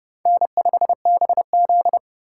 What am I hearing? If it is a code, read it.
Morse code, heard as a keyed tone, about 29 wpm, 700 Hz. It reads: N5BZ